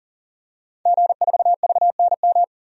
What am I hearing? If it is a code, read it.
Morse code, heard as a keyed tone, about 40 wpm, 700 Hz. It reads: G4VNM
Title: DXCC-100 Call Signs - 5 Characters - 40wpm